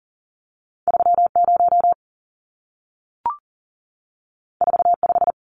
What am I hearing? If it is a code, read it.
Morse code, heard as a keyed tone, about 40 wpm, 700 Hz. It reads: 30 45
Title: Numbers - 2-Digit 40wpm